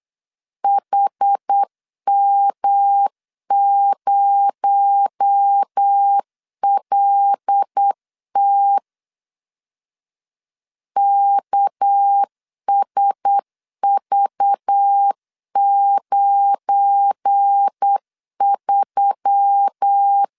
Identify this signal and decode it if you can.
Morse code, heard as a keyed tone, about 8 wpm, 800 Hz. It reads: HM0LT KSV93
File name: LEZIONI / 7 GF9Z / 7 GF9Z 8wpm 3213